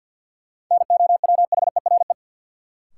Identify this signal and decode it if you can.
Morse code, heard as a keyed tone, about 53 wpm, 700 Hz. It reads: NOWHERE
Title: Sets of 5 Words Encore - New Word Drill - 50wpm